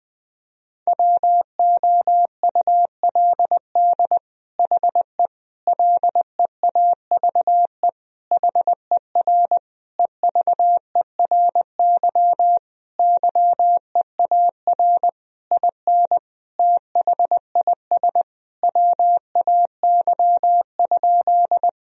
Morse code, heard as a keyed tone, about 20 wpm, 700 Hz. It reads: WOULD HE LEAVE HER EVERY YEAR IN THIS WAY?
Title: Sentences from Top 1000 Words - Review Part 1 - 20wpm